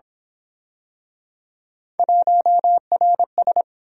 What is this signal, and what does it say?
Morse code, heard as a keyed tone, about 26 wpm, 700 Hz. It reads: 1RS
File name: Speed-Racing - Three Letter-Number Combinations - 22wpm